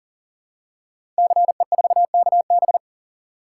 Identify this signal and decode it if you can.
Morse code, heard as a keyed tone, about 42 wpm, 700 Hz. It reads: CE4KB
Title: Code Groups - 5 Letter:Numbers - 40wpm